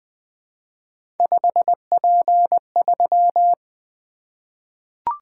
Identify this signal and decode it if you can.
Morse code, heard as a keyed tone, about 20 wpm, 700 Hz. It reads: 5P3
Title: Three Letter-Number Combinations - 20wpm